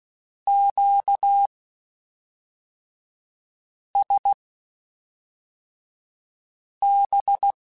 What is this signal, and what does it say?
Morse code, heard as a keyed tone, about 16 wpm, spaced as for 3 wpm, 800 Hz. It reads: QSB
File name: (spaced) / Q-Calls Spaced-mp3s / Q Calls Twice sp @5